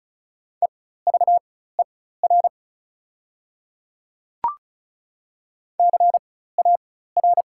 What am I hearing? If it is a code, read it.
Morse code, heard as a keyed tone, about 36 wpm, spaced as for 16 wpm, 690 Hz. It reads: EVER CAR
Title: 4x Character Spacing - Top 500 Words - 35wpm